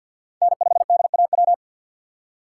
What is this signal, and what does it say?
Morse code, heard as a keyed tone, about 53 wpm, 690 Hz. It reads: N5DAW
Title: USA Call Signs - 5 Characters - 6x Character Spacing - 50wpm